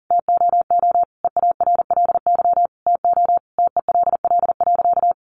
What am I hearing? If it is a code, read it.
Morse code, heard as a keyed tone, about 40 wpm, 700 Hz. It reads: TOO EARLY TO TELL.
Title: A Sound Of Thunder 40wpm